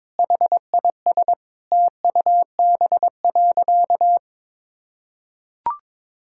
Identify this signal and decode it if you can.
Morse code, heard as a keyed tone, about 22 wpm, 700 Hz. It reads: HIS TUB.
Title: Pangram Challenge v2 - 22wpm